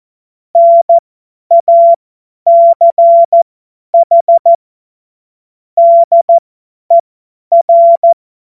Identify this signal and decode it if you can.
Morse code, heard as a keyed tone, about 14 wpm, spaced as for 10 wpm, 680 Hz. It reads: NACH DER
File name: Morsekurs / Witze / Witz 18. Juli 2023 50 F70